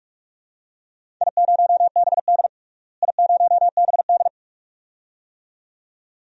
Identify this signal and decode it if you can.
Morse code, heard as a keyed tone, about 46 wpm, 700 Hz. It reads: I0BD I0BD
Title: DXCC-100 Call Signs - 4 Characters - Repeated 2x - 45wpm